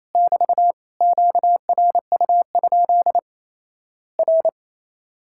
Morse code, heard as a keyed tone, about 28 wpm, 700 Hz. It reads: <BT> QRU? R
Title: Worked All Europe DX Contest - QTC Exchange - 28wpm